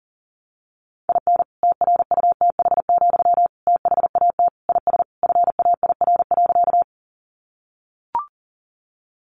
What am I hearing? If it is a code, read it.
Morse code, heard as a keyed tone, about 40 wpm, 700 Hz. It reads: IN TRUTH, THAT IS FAIR.
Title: Sentences from Top 1300 Words 40wpm